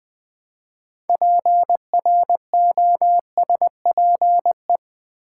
Morse code, heard as a keyed tone, about 20 wpm, 700 Hz. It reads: PROSPE
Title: Build A Word - 2 Letters At A Time - 8-Letter Words - 20wpm